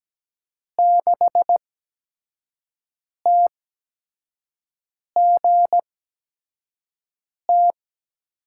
Morse code, heard as a keyed tone, about 17 wpm, 700 Hz. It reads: 6 T G T
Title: USA Call Signs - 5 Characters - 8x Character Spacing - 17wpm